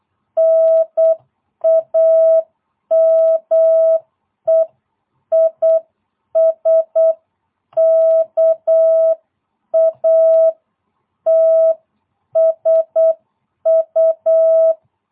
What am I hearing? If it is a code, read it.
Morse code, heard as a keyed tone, about 8 wpm, 650 Hz. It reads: NAME IS KATSU